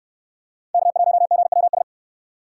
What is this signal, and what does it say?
Morse code, heard as a keyed tone, about 65 wpm, 700 Hz. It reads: K1GWS